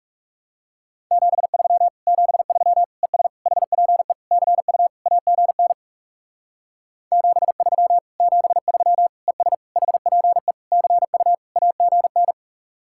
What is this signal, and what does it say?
Morse code, heard as a keyed tone, about 37 wpm, 700 Hz. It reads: 73 73 ES HPE CU AGN 73 73 ES HPE CU AGN
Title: Speed-Racing Common QSO Phrases - 30wpm